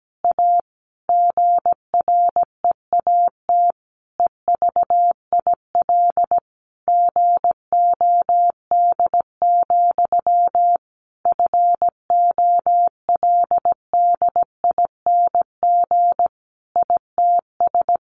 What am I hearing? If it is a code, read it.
Morse code, heard as a keyed tone, about 17 wpm, 700 Hz. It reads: A GREAT EVIL GOD, FOLDING ITS
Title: A Sound Of Thunder 17wpm